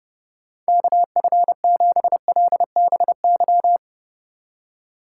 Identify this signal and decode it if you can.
Morse code, heard as a keyed tone, about 29 wpm, 700 Hz. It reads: KF7LBY
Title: USA Call Signs - 6 Characters - 4x Character Spacing - 30wpm